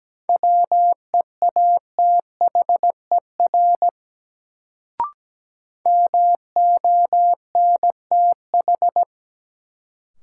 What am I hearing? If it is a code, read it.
Morse code, heard as a keyed tone, about 17 wpm, 700 Hz. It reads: WEATHER MONTH
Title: Sentences from Top 800 Words - New Word Drill - 17wpm